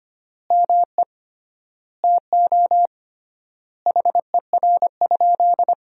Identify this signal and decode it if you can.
Morse code, heard as a keyed tone, about 25 wpm, 700 Hz. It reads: ME TO HER?
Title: 3x Word-Spacing - Sentences from Top 200 Words - 25wpm